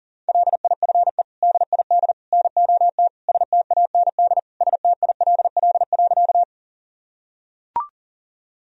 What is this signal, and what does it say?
Morse code, heard as a keyed tone, about 40 wpm, 700 Hz. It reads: LIFE DID NOT STAND STILL.
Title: Sentences from Top 1000 Words - Review Part 1 - 40wpm